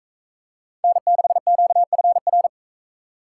Morse code, heard as a keyed tone, about 40 wpm, 700 Hz. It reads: N6QFR